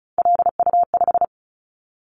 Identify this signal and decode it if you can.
Morse code, heard as a keyed tone, about 34 wpm, 700 Hz. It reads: LU5
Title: Callsign Prefixes for Top 100 DXCC - 35wpm